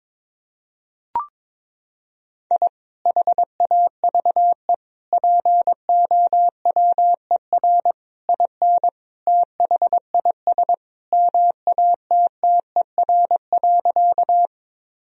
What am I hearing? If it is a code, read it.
Morse code, heard as a keyed tone, about 22 wpm, 700 Hz. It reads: I HAVE POWER IN THIS MATTER.
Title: Sentences from Top 1000 Words - Review Part 2 - 22wpm